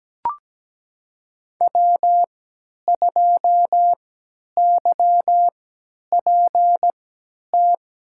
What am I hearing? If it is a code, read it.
Morse code, heard as a keyed tone, about 17 wpm, spaced as for 10 wpm, 700 Hz. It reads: W2YPT
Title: USA Call Signs - 5 Characters - 3x Character Spacing - 17wpm